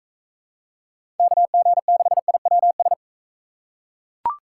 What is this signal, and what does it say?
Morse code, heard as a keyed tone, about 40 wpm, 700 Hz. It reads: KG6IWS